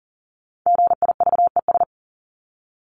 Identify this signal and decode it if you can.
Morse code, heard as a keyed tone, about 40 wpm, 700 Hz. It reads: GIVES
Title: ES:S Suffix - Top 100 Words with Base Words 40wpm